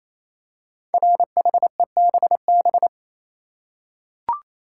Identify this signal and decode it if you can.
Morse code, heard as a keyed tone, about 29 wpm, 700 Hz. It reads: RHEBB
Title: Code Groups - 5 Letters - 28wpm